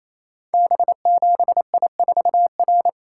Morse code, heard as a keyed tone, about 27 wpm, 700 Hz. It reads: B7I4R
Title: Code Groups - 5 Letter:Numbers - 28wpm